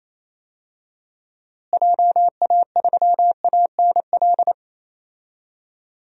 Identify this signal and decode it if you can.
Morse code, heard as a keyed tone, about 27 wpm, 700 Hz. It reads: JA3ANL
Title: DXCC-100 Call Signs - 6 Characters - Repeated 2x - 28wpm